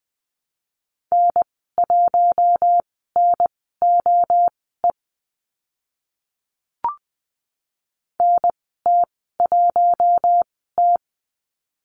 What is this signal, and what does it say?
Morse code, heard as a keyed tone, about 20 wpm, spaced as for 14 wpm, 700 Hz. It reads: N1NOE NT1T
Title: 042 Lesson 8g - Number 1 - Call Signs 20wpm